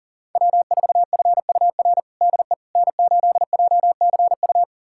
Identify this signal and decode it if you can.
Morse code, heard as a keyed tone, about 40 wpm, 700 Hz. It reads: W4FUR DE N8JCU
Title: USA Callsign DE Callsign - 5 Characters - 40wpm